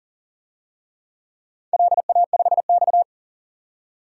Morse code, heard as a keyed tone, about 43 wpm, 690 Hz. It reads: LA5X
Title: DXCC-100 Call Signs - 4 Characters - 40wpm